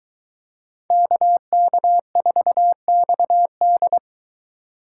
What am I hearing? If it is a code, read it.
Morse code, heard as a keyed tone, about 23 wpm, 700 Hz. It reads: KK4XD